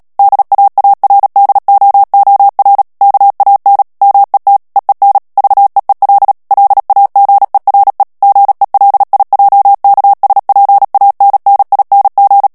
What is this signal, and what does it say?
Morse code, heard as a keyed tone, about 37 wpm, 800 Hz. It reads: DAARDOOR KAN MET EEN VEEL LAGERE GELIJKSPANNING
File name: morse mp3 / HOLLANDAIS texte clair 2 vitesse 35 mots minute